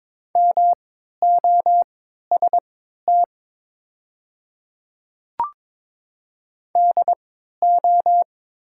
Morse code, heard as a keyed tone, about 21 wpm, spaced as for 12 wpm, 700 Hz. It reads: MOST DO
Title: 3x Character Spacing - Top 100 Words - 22wpm